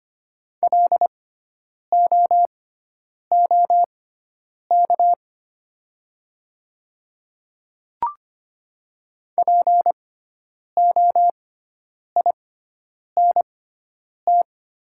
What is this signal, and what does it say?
Morse code, heard as a keyed tone, about 25 wpm, spaced as for 9 wpm, 700 Hz. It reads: LOOK POINT
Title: 6x Character Spacing - Top 200 Words - 25wpm